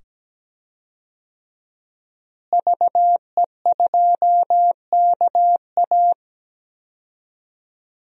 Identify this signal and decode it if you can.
Morse code, heard as a keyed tone, about 17 wpm, 700 Hz. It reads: VE2KA
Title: DXCC-100 Call Signs - 5 Characters - Repeated 2x - 17wpm